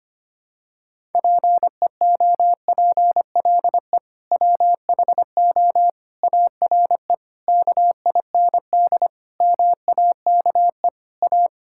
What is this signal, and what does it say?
Morse code, heard as a keyed tone, about 25 wpm, 700 Hz. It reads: PEOPLE WHO ARE KIND MAKE A
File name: practice / 25wpm / Sentences with 3-Word Phrases - 25wpm